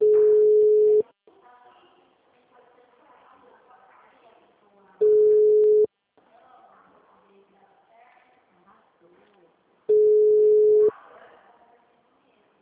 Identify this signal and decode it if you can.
Ringback tone (Europe, 425 Hz), with speech.